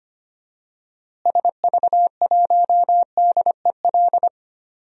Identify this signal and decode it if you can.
Morse code, heard as a keyed tone, about 25 wpm, 700 Hz. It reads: SV1DEL
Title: DXCC-100 Call Signs - 6 Characters - Repeated 2x - 25wpm